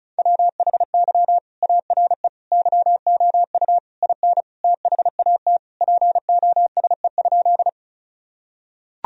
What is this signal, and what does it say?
Morse code, heard as a keyed tone, about 34 wpm, 700 Hz. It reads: WHY ARE YOU IN THAT POSE?
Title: Sentences from Top 1000 Words - Review Part 2 - 35wpm